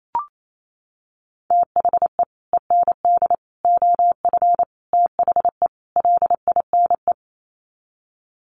Morse code, heard as a keyed tone, about 28 wpm, 700 Hz. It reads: THE END OF THE LINE
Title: Sets of 5 Words Encore2 28wpm